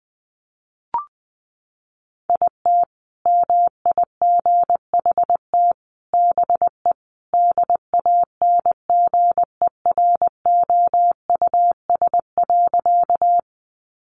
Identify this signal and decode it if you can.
Morse code, heard as a keyed tone, about 20 wpm, 700 Hz. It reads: IT MIGHT BE DANGEROUS.
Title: Sentences from Top 1200 Words 20wpm